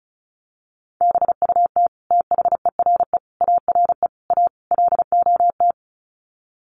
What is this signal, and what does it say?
Morse code, heard as a keyed tone, about 34 wpm, 700 Hz. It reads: BUT THERE ARE A LOT
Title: Sets of 5 Words Encore2 35wpm